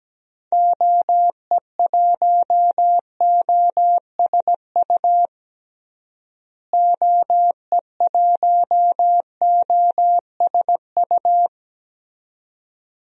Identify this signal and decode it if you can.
Morse code, heard as a keyed tone, about 17 wpm, 700 Hz. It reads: OE1OSU OE1OSU
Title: DXCC-100 Call Signs - 6 Characters - Repeated 2x - 17wpm